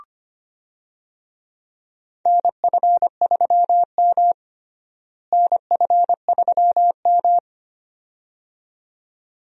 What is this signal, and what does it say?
Morse code, heard as a keyed tone, about 25 wpm, 700 Hz. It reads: NF3M NF3M
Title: USA Call Signs - 4 Characters - Repeated 2x - 25wpm